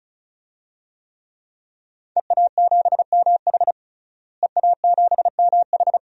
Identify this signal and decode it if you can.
Morse code, heard as a keyed tone, about 34 wpm, 700 Hz. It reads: EA7MH EA7MH
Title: DXCC-100 Call Signs - 5 Characters - Repeated 2x - 35wpm